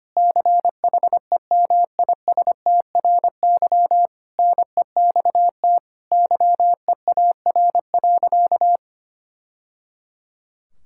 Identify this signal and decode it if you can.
Morse code, heard as a keyed tone, about 25 wpm, 700 Hz. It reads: CHEMISTRY NEXT YEAR.
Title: Sentences from Top 1000 Words - Review Part 6 - 25wpm